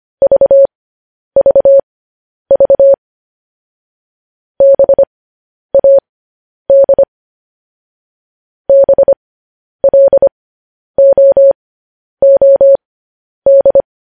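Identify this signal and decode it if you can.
Morse code, heard as a keyed tone, about 25 wpm, spaced as for 10 wpm, 550 Hz. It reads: VVV BAD BLOOD
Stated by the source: Morse code transcription of news headlines sent at 10 wpm.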